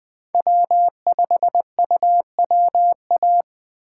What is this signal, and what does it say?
Morse code, heard as a keyed tone, about 20 wpm, 700 Hz. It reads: W5UWA
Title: USA Call Signs - 5 Characters - 6x Character Spacing - 20wpm